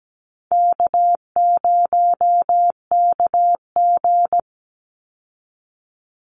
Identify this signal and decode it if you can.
Morse code, heard as a keyed tone, about 17 wpm, 700 Hz. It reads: K0KG
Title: Easy Call Signs Repeated 2x - 17wpm